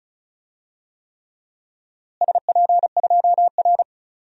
Morse code, heard as a keyed tone, about 36 wpm, 700 Hz. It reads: SP2R
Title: DXCC-100 Call Signs - 4 Characters - Repeated 3x - 35wpm